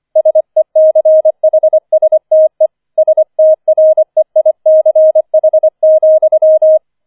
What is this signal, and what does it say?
Morse code, heard as a keyed tone, about 25 wpm, 630 Hz. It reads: SECHSTE STREICH,
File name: 25wpm